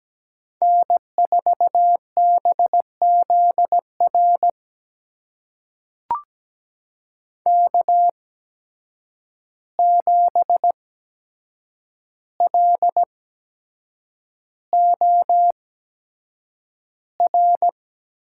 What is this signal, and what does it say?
Morse code, heard as a keyed tone, about 17 wpm, spaced as for 7 wpm, 700 Hz. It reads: N4BZR K7LOR
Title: USA Call Signs - 5 Characters - 8x Character Spacing - 17wpm